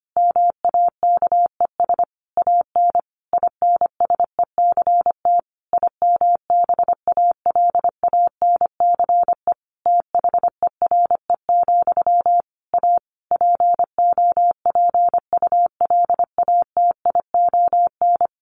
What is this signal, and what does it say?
Morse code, heard as a keyed tone, about 25 wpm, 700 Hz. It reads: MAKES AN INSECT IMBALANCE THERE, A POPULATION
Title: A Sound Of Thunder 25wpm